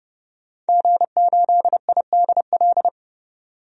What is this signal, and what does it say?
Morse code, heard as a keyed tone, about 30 wpm, 700 Hz. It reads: G8IDL
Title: DXCC-100 Call Signs - 5 Characters - 30wpm